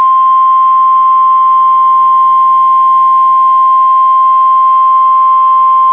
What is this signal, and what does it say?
Emergency Alert System (SAME) tones (the 1050 Hz attention tone). It is heard from 0.8 to 5.3 s.